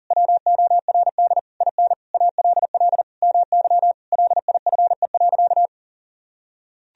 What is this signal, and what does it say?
Morse code, heard as a keyed tone, about 40 wpm, 700 Hz. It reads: WORD IN ALL MY LIFE.
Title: Sentences from Top 1000 Words - Review Part 1 - 40wpm